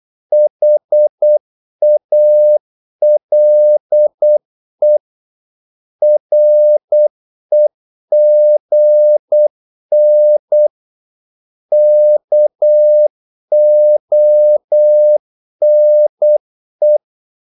Morse code, heard as a keyed tone, about 8 wpm, 600 Hz. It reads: HALE REGN KONE
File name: Korte ord DK 8wpm